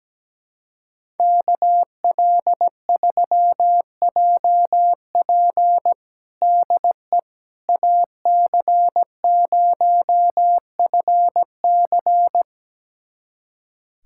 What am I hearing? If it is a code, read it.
Morse code, heard as a keyed tone, about 17 wpm, 700 Hz. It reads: KL3JP DE AC0FC
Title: USA Callsign DE Callsign - 5 Characters - 17wpm